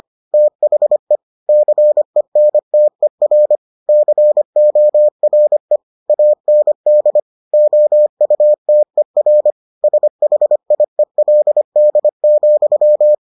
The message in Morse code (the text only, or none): THE CENTER CORE AND OUTER SHIELD,